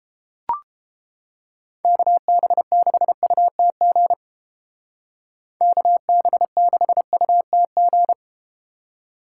Morse code, heard as a keyed tone, about 32 wpm, 700 Hz. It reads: KB6UTG KB6UTG